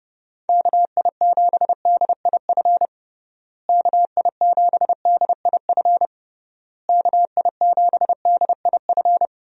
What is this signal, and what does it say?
Morse code, heard as a keyed tone, about 30 wpm, 700 Hz. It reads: KI7DIF KI7DIF KI7DIF
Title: USA Call Signs - 6 Characters - Repeated 3x - 30wpm